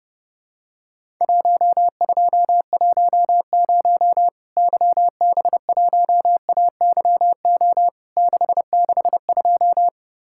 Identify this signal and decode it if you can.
Morse code, heard as a keyed tone, about 30 wpm, 700 Hz. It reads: 1210 YB1AYO 662
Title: Worked All Europe DX Contest - QTC Exchange - 30wpm